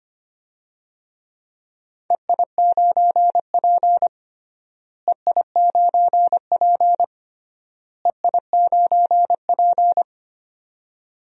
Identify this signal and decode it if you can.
Morse code, heard as a keyed tone, about 25 wpm, 700 Hz. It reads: EI9P EI9P EI9P
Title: DXCC-100 Call Signs - 4 Characters - Repeated 3x - 25wpm